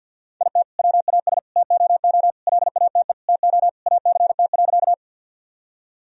Morse code, heard as a keyed tone, about 46 wpm, 700 Hz. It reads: IT WAS TOO LATE TO ACT.
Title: Sentences from Top 1000 Words - Review Part 1 - 50wpm